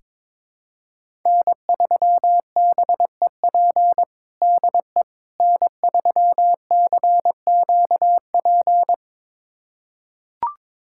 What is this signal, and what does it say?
Morse code, heard as a keyed tone, about 22 wpm, 700 Hz. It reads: N3BEP DE N3CQP
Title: USA Callsign DE Callsign - 5 Characters - 22wpm